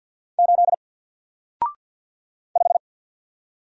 Morse code, heard as a keyed tone, about 50 wpm, 700 Hz. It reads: 8 5
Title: Single Character (Letter:Number:Punctuation:Prosign) Alt 50wpm